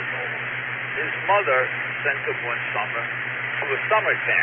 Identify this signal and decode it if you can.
Emergency Alert System (SAME) tones, heard as harsh buzzing bursts, with speech. It is heard from 0.2 to 3.5 s.